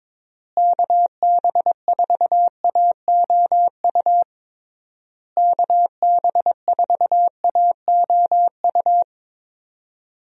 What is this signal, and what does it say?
Morse code, heard as a keyed tone, about 22 wpm, 700 Hz. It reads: KB4AOU KB4AOU
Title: USA Call Signs - 6 Characters - Repeated 3x - 22wpm